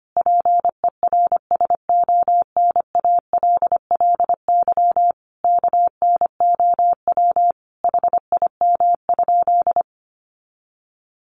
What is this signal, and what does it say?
Morse code, heard as a keyed tone, about 25 wpm, 700 Hz. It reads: PERSONALLY KNOW HIM?
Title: Sentences from Top 1400 Words 25wpm